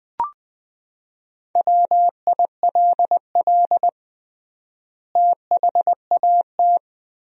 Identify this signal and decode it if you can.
Morse code, heard as a keyed tone, about 19 wpm, 700 Hz. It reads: WILL THAT
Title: 3x Word-Spacing - Sentences from Top 100 Words - 20wpm